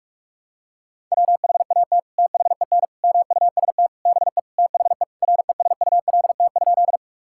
Morse code, heard as a keyed tone, about 46 wpm, 700 Hz. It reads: WHAT THEN MUST BE THE RESULT?
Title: Sentences from Top 1000 Words - Review Part 5 - 45wpm